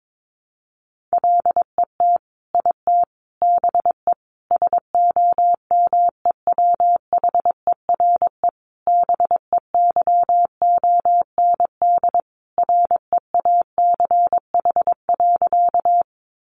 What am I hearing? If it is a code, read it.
Morse code, heard as a keyed tone, about 22 wpm, 700 Hz. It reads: LET IT BE SOMEWHERE BEYOND REACH.
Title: Sentences from Top 1400 Words 22wpm